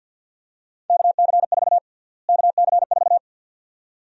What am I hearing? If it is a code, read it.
Morse code, heard as a keyed tone, about 52 wpm, 700 Hz. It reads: KC4 KC4
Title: USA Call Sign Prefixes - Repeated 2x - 50wpm